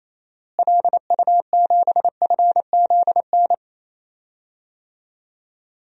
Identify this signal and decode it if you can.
Morse code, heard as a keyed tone, about 27 wpm, 700 Hz. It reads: LU7FZN
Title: DXCC-100 Call Signs - 6 Characters - Repeated 2x - 28wpm